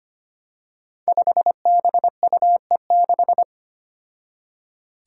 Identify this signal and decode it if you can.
Morse code, heard as a keyed tone, about 25 wpm, 700 Hz. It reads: 5BUE6
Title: Five Challenging Letter-Number Combinations 25wpm